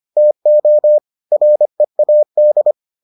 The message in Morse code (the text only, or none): TO READ